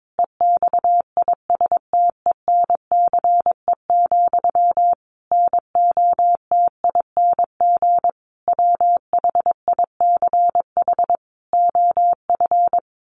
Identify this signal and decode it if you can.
Morse code, heard as a keyed tone, about 22 wpm, 700 Hz. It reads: EXISTENCE, NOTING WHICH OF
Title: A Sound Of Thunder 22wpm